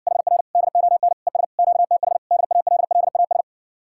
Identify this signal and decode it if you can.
Morse code, heard as a keyed tone, about 60 wpm, 700 Hz. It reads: HR DON ES QTH DALLAS